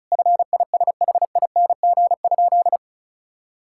Morse code, heard as a keyed tone, about 34 wpm, 700 Hz. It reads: FISHING?
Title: Sentences from Top 1000 Words - Review Part 5 - 35wpm